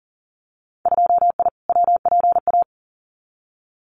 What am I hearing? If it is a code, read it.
Morse code, heard as a keyed tone, about 38 wpm, 700 Hz. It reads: 2I WPA
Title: Winter Field Day Exchanges 40wpm